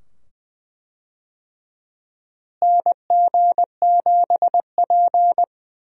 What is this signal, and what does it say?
Morse code, heard as a keyed tone, about 20 wpm, 700 Hz. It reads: NG7P
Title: USA Call Signs - 4 Characters - Repeated 2x - 20wpm